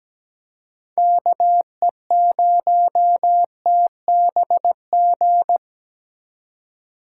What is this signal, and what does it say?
Morse code, heard as a keyed tone, about 17 wpm, 700 Hz. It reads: KE0TBG
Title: USA Call Signs - 6 Characters - 8x Character Spacing - 17wpm